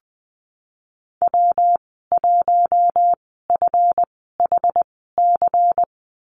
Morse code, heard as a keyed tone, about 20 wpm, spaced as for 15 wpm, 700 Hz. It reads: W1FHC
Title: 186 Lesson 30g - Letter b - Call Signs 20wpm